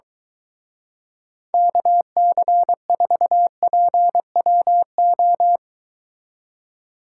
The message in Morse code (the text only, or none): KC4PWO